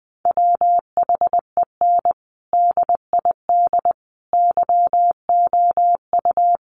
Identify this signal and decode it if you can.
Morse code, heard as a keyed tone, about 20 wpm, 700 Hz. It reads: WHEN DID YOU
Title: Sentences from Top 300 Words 20wpm